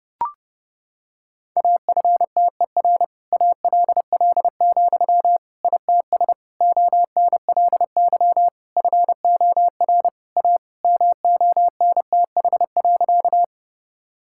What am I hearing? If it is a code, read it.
Morse code, heard as a keyed tone, about 30 wpm, 700 Hz. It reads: AFTER ALL, ITS ONLY FOR A MONTH.
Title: Sentences from Top 1000 Words - Review Part 5 - 30wpm